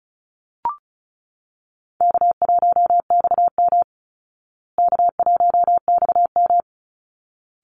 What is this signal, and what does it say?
Morse code, heard as a keyed tone, about 34 wpm, 700 Hz. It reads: K1XM K1XM
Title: Easy Call Signs Repeated 2x - 35wpm